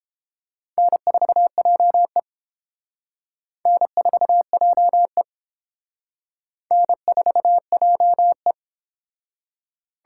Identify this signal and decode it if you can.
Morse code, heard as a keyed tone, about 31 wpm, 700 Hz. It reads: N4JE N4JE N4JE